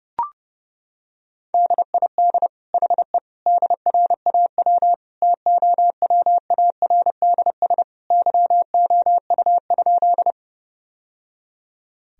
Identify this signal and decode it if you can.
Morse code, heard as a keyed tone, about 30 wpm, 700 Hz. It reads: DID HE DRAW TOWARDS YOU?
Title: Sentences from Top 1000 Words - Review Part 2 - 30wpm